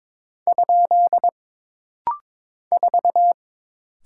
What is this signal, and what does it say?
Morse code, heard as a keyed tone, about 23 wpm, 700 Hz. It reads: ? 4
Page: This practice set is composed of Letters, Numbers, common punctuations, and common prosigns.